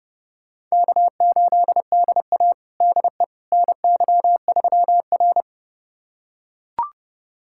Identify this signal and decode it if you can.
Morse code, heard as a keyed tone, about 30 wpm, 700 Hz. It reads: K8DA DE NY3R
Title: USA Callsign DE Callsign - 4 Characters - 30wpm